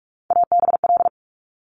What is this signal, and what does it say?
Morse code, heard as a keyed tone, about 46 wpm, 700 Hz. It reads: ABL
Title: Top Three Letter Combinations in Words 45wpm